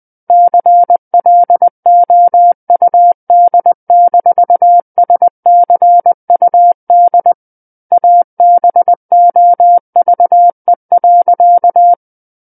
Morse code, heard as a keyed tone, about 20 wpm, 700 Hz. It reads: CLOUD-SCUD ABOVE.
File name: 20 WPM